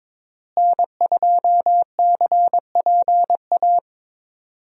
Morse code, heard as a keyed tone, about 22 wpm, 700 Hz. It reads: N2CPA
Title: USA Call Signs - 5 Characters - Repeated 3x - 22wpm